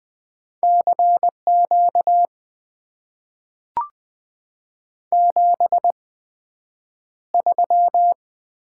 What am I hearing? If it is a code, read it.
Morse code, heard as a keyed tone, about 20 wpm, spaced as for 9 wpm, 700 Hz. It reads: CQ 73
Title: Top 90 QSO Elements - 8x Character Spacing - 20wpm